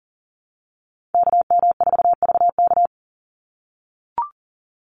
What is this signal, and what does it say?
Morse code, heard as a keyed tone, about 40 wpm, 700 Hz. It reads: KM4VK
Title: USA Call Signs 40wpm